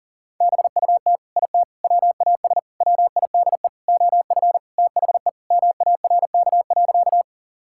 Morse code, heard as a keyed tone, about 40 wpm, 700 Hz. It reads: BUT IT WAS WIDE OF THE MARK.
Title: Sentences from Top 1000 Words - Review Part 6 - 40wpm